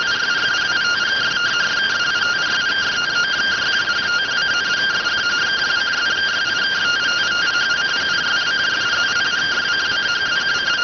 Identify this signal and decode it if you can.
Two-tone teleprinter signal, about 170 Hz shift.